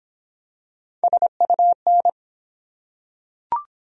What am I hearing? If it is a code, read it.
Morse code, heard as a keyed tone, about 22 wpm, 700 Hz. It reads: SUN